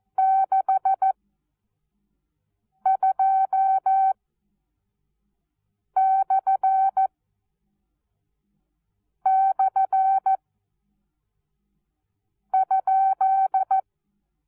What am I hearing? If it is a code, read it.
Morse code, heard as a keyed tone, about 14 wpm, spaced as for 4 wpm, 770 Hz. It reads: 62//?
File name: Onlinelehrgang / morsen / DJ4UF Morsen19